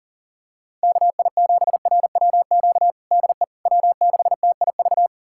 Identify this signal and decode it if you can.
Morse code, heard as a keyed tone, about 40 wpm, 700 Hz. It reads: KI7RWQ DE W6TIV
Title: USA Callsign DE Callsign 40wpm